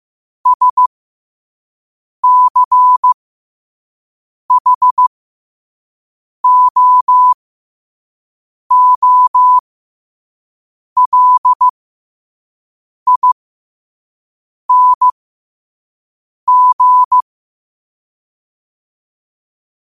Morse code, heard as a keyed tone, about 15 wpm, spaced as for 5 wpm, 1000 Hz. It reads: SCHOOLING
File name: qotdfiles / 5wpm